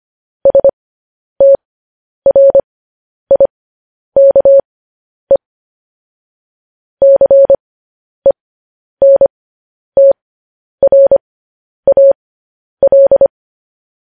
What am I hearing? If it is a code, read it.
Morse code, heard as a keyed tone, about 25 wpm, spaced as for 10 wpm, 550 Hz. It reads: STRIKE CENTRAL